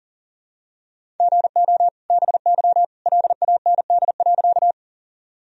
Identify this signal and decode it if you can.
Morse code, heard as a keyed tone, about 40 wpm, 700 Hz. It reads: GO BY LAND.
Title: Sentences from Top 1000 Words - Review Part 1 - 40wpm